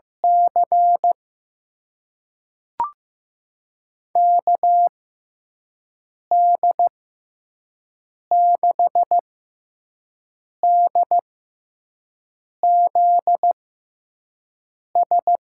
Morse code, heard as a keyed tone, about 15 wpm, spaced as for 5 wpm, 700 Hz. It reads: C KD6DZS
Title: USA Call Signs - 6 Characters - 6x Character Spacing - 15wpm